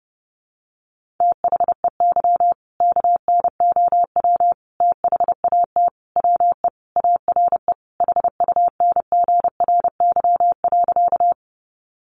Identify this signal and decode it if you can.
Morse code, heard as a keyed tone, about 30 wpm, 700 Hz. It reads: THEY KNOW THAT WE ARE HUNGRY.
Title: Sentences from Top 1400 Words 30wpm